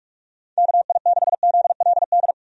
Morse code, heard as a keyed tone, about 46 wpm, 700 Hz. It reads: KI6ZLD
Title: USA Call Signs - 6 Characters - 45wpm